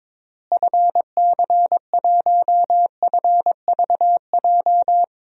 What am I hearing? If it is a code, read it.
Morse code, heard as a keyed tone, about 22 wpm, 700 Hz. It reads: FC1FVJ
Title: DXCC-100 Call Signs - 6 Characters - Repeated 2x - 22wpm